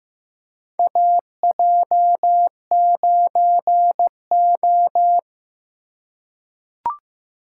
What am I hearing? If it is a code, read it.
Morse code, heard as a keyed tone, about 15 wpm, 700 Hz. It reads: AJ9O
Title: USA Call Signs - 4 Characters - 2x Character Spacing - 15wpm